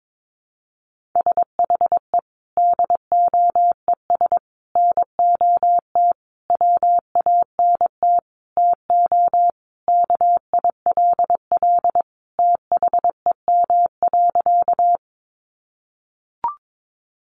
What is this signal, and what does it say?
Morse code, heard as a keyed tone, about 22 wpm, 700 Hz. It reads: SHE DOES NOT WANT TO KILL THEM.
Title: Easy Phrases 22wpm